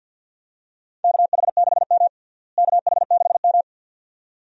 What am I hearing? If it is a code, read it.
Morse code, heard as a keyed tone, about 52 wpm, 700 Hz. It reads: KH6M KH6M
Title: USA Call Signs - 4 Characters - Repeated 2x - 50wpm